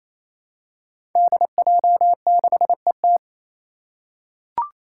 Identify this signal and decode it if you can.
Morse code, heard as a keyed tone, about 28 wpm, 700 Hz. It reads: DJ6ET
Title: Five Challenging Letter-Number Combinations 28wpm